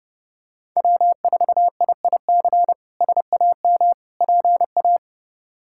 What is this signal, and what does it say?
Morse code, heard as a keyed tone, about 30 wpm, 700 Hz. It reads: W4IIC SAM PA
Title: K1USN SST QSO Practice - 30wpm